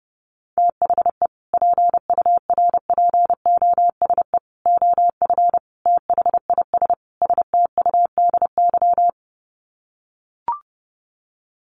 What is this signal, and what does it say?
Morse code, heard as a keyed tone, about 30 wpm, 700 Hz. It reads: THE PURPOSE OF THIS STUDY
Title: Sets of 5 Words 30wpm